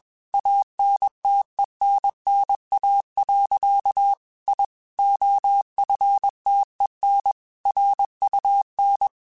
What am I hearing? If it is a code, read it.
Morse code, heard as a keyed tone, about 21 wpm, 800 Hz. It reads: ANTENNA. I OFTEN RUN
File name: QSOs / QSOs-mp3s / kc7haa@20